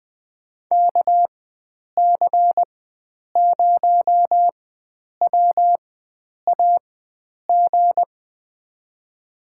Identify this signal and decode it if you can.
Morse code, heard as a keyed tone, about 20 wpm, spaced as for 9 wpm, 700 Hz. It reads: KC0WAG
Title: USA Call Signs - 6 Characters - 4x Character Spacing - 20wpm